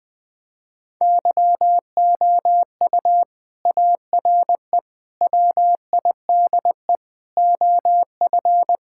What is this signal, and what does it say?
Morse code, heard as a keyed tone, about 20 wpm, 700 Hz. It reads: YOU ARE WIDE OF
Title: Sentences from Top 1000 Words - Review Part 6 - 20wpm